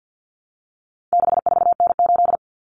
Morse code, heard as a keyed tone, about 50 wpm, 700 Hz. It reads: 64N8
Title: Common Cut Numbers - 4-Digit 50wpm